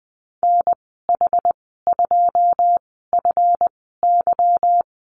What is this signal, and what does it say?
Morse code, heard as a keyed tone, about 20 wpm, spaced as for 15 wpm, 700 Hz. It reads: NH2FY
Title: 179 Lesson 29g - Slash - Call Signs 20wpm